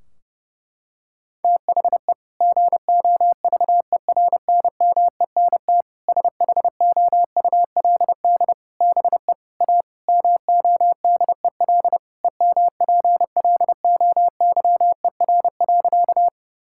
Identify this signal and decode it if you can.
Morse code, heard as a keyed tone, about 32 wpm, 700 Hz. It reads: THE GOVERNMENT SHOULD BE A MODEL EMPLOYER.
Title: Sentences from Top 1600 Words 30wpm